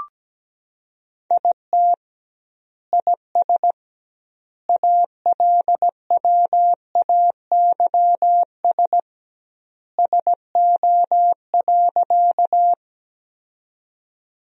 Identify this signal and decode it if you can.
Morse code, heard as a keyed tone, about 17 wpm, 700 Hz. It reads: IT IS ALWAYS SO.
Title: 2x Word-Spacing - Sentences from Top 300 Words - 17wpm